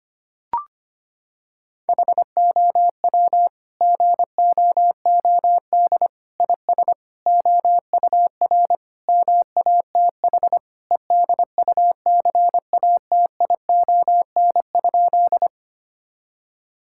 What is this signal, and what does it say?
Morse code, heard as a keyed tone, about 25 wpm, 700 Hz. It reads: HOW GOOD IS OUR MATH EDUCATION?
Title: Sentences from Top 1000 Words - Review Part 6 - 25wpm